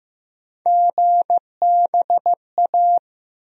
Morse code, heard as a keyed tone, about 15 wpm, 700 Hz. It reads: GBA
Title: USA Call Sign Postfixes - 3 Letters 15wpm